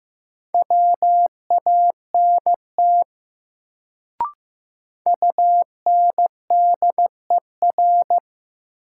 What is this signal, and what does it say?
Morse code, heard as a keyed tone, about 15 wpm, 700 Hz. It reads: WANT UNDER
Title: Sets of 2 Words - New Word Drill - 15wpm